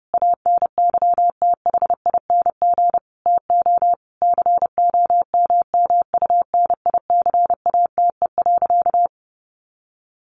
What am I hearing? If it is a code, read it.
Morse code, heard as a keyed tone, about 30 wpm, 700 Hz. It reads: ANYTHING TO COMMUNICATE.
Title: Sentences from Top 1300 Words 30wpm